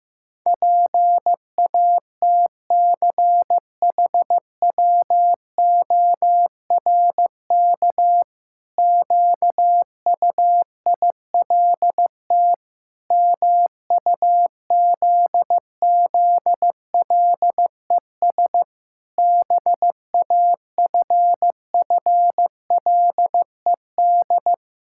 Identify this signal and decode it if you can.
Morse code, heard as a keyed tone, about 15 wpm, 700 Hz. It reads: PATCHWORK QUILT MUZZLES BAFFLED
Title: Pangram Challenge v2 - 15wpm